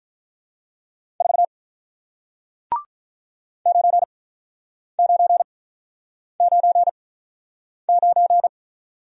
Morse code, heard as a keyed tone, about 47 wpm, spaced as for 9 wpm, 700 Hz. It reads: 4 9999